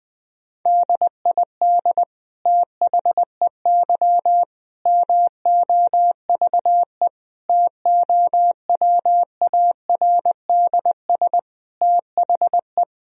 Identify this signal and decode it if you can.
Morse code, heard as a keyed tone, about 20 wpm, 700 Hz. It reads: DID THEY MOVE TOWARDS THE
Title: Sentences from Top 1000 Words - Review Part 2 - 20wpm